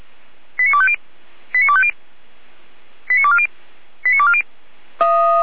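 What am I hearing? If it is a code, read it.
Five-tone selective call (ZVEI), heard as quick runs of five tones, tones of about 70 ms. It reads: 89250 89250 89240 89240
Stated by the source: ZVEI five-tone selective call 89276 received by SDR, logged by the Feuerwehr Orlamünde (fire brigade) audio server.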